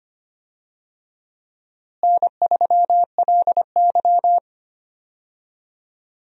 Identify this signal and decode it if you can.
Morse code, heard as a keyed tone, about 25 wpm, 700 Hz. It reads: N3LY
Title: USA Call Signs - 4 Characters - Repeated 2x - 25wpm